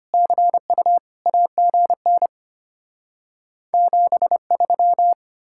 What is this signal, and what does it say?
Morse code, heard as a keyed tone, about 33 wpm, 700 Hz. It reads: CU AGN 73